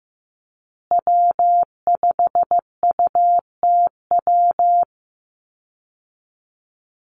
Morse code, heard as a keyed tone, about 15 wpm, 700 Hz. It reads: W5UTW
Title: Call Signs Repeated 3x - 15wpm